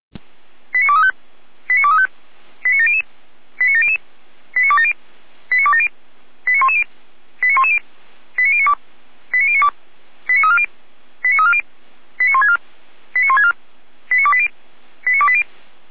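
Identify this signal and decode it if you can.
Five-tone selective call (ZVEI), heard as quick runs of five tones, tones of about 80 ms. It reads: 89235 89235 89700 89700 89270 89270 89110 89110 89092 89092 89340 89340 89164 89164 89280 89280
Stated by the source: ZVEI five-tone selective call, received on SDR.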